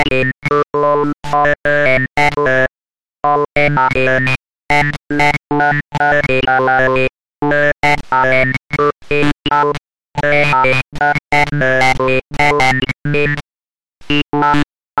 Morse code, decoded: none